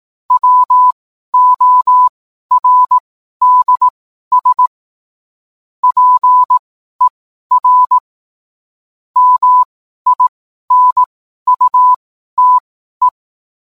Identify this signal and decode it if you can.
Morse code, heard as a keyed tone, about 18 wpm, spaced as for 12 wpm, 1000 Hz. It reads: WORDS PER MINUTE